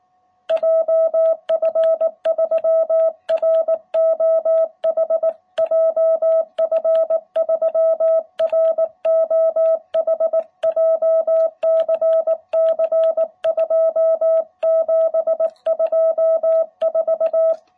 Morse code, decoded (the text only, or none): JF3ROH JF3ROH JCC 2724